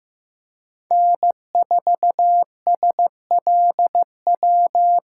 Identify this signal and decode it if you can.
Morse code, heard as a keyed tone, about 15 wpm, 700 Hz. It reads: N4SLW
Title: USA Call Signs - 6x Character Spacing - 15wpm